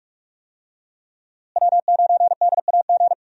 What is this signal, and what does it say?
Morse code, heard as a keyed tone, about 41 wpm, 700 Hz. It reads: W9DAG
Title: USA Call Signs - 5 Characters - Repeated 3x - 45wpm